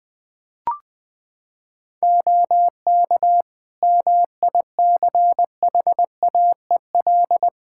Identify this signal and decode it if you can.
Morse code, heard as a keyed tone, about 21 wpm, 700 Hz. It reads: OK MICHAEL
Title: K1USN SST QSO Practice 20wpm